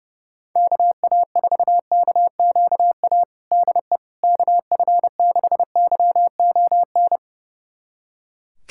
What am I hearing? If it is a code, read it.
Morse code, heard as a keyed tone, about 30 wpm, 700 Hz. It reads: KA4KQA DE KF6YON
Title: USA Callsign DE Callsign 30wpm